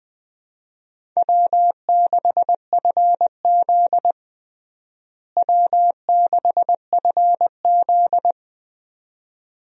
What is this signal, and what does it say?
Morse code, heard as a keyed tone, about 20 wpm, 700 Hz. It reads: W6FZ W6FZ
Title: USA Call Signs - 4 Characters - Repeated 2x - 20wpm